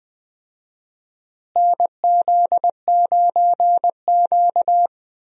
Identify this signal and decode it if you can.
Morse code, heard as a keyed tone, about 19 wpm, 700 Hz. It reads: NZ9Q
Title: USA Call Signs - 4 Characters - Repeated 3x - 20wpm